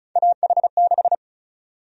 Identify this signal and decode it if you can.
Morse code, heard as a keyed tone, about 36 wpm, 700 Hz. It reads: AH6
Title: USA Call Sign Prefixes - Repeated 2x - 35wpm